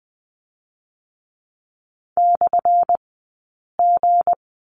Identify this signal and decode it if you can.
Morse code, heard as a keyed tone, about 20 wpm, spaced as for 8 wpm, 700 Hz. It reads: /G
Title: 197 Lesson 32d - Letter k - Two Characters 20wpm